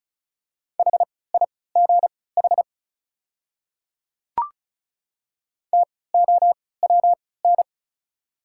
Morse code, heard as a keyed tone, about 36 wpm, spaced as for 20 wpm, 690 Hz. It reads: HIGH TOWN
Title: 3x Character Spacing - Top 500 Words - 35wpm